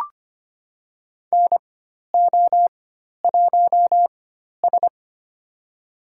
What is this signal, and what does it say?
Morse code, heard as a keyed tone, about 25 wpm, spaced as for 12 wpm, 700 Hz. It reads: NO1S
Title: USA Call Signs - 4 Characters - 4x Character Spacing - 25wpm